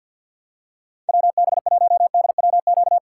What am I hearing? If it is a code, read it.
Morse code, heard as a keyed tone, about 53 wpm, 700 Hz. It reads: WB1DWQ
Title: USA Call Signs - 6 Characters - 6x Character Spacing - 50wpm